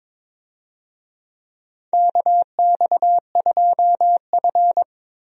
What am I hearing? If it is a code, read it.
Morse code, heard as a keyed tone, about 22 wpm, 700 Hz. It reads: KX2F
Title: USA Call Signs - 4 Characters - Repeated 2x - 22wpm